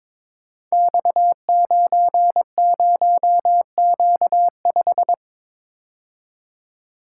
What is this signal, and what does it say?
Morse code, heard as a keyed tone, about 23 wpm, 700 Hz. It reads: X90Q5
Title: Code Groups - 5 Letter:Numbers - 22wpm